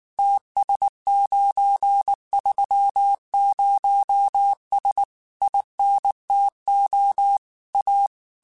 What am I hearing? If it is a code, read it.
Morse code, heard as a keyed tone, about 19 wpm, 800 Hz. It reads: TS930S INTO A
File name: QSOs / QSOs-mp3s / kc7haa@18